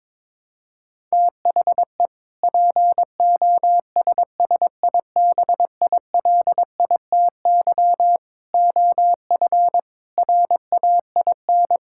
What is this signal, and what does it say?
Morse code, heard as a keyed tone, about 21 wpm, 700 Hz. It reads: THE POSSIBILITY OF RAIN
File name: practice / 22wpm / Sentences with 3-Word Phrases - 22wpm